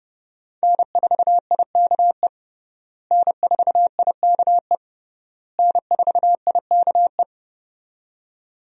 Morse code, heard as a keyed tone, about 30 wpm, 700 Hz. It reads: N4IKE N4IKE N4IKE
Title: USA Call Signs - 5 Characters - Repeated 3x - 30wpm